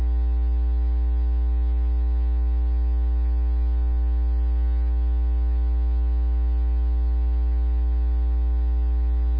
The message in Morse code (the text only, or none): none